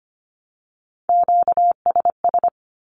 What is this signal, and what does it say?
Morse code, heard as a keyed tone, about 25 wpm, 700 Hz. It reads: QSS
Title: Q-Codes Repeated 3x 25wpm